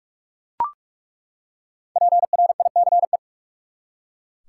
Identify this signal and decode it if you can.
Morse code, heard as a keyed tone, about 43 wpm, 700 Hz. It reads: PRICE
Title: Five Letter Words - 45wpm